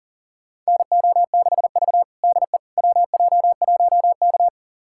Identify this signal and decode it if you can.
Morse code, heard as a keyed tone, about 37 wpm, 700 Hz. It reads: NO6V DE WJ1K
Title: USA Callsign DE Callsign - 4 Characters - 40wpm